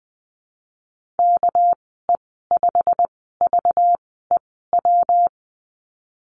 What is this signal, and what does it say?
Morse code, heard as a keyed tone, about 20 wpm, spaced as for 15 wpm, 700 Hz. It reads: KE5VEW
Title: 200 Lesson 32g - Letter k - Call Signs 20wpm